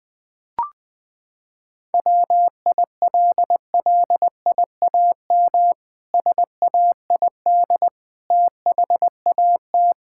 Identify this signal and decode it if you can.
Morse code, heard as a keyed tone, about 21 wpm, 700 Hz. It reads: WILLIAM SAID THAT
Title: Pangram Challenge v2 - 20wpm